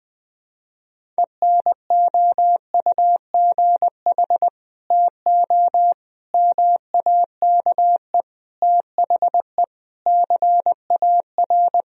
Morse code, heard as a keyed tone, about 21 wpm, 700 Hz. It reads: ENOUGH TO MAKE THE CAR
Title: Sentences from Top 1000 Words - Review Part 1 - 20wpm